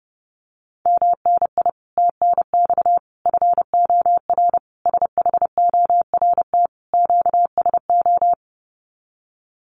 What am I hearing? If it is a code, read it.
Morse code, heard as a keyed tone, about 30 wpm, 700 Hz. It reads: MNI TNX FOR SHORT QSO
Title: Common QSO Phrases 30wpm